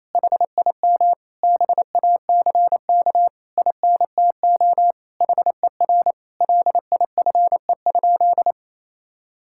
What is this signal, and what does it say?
Morse code, heard as a keyed tone, about 27 wpm, 700 Hz. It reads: HIM BACK INTO HER LIFE?
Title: Sentences from Top 1000 Words - Review Part 6 - 28wpm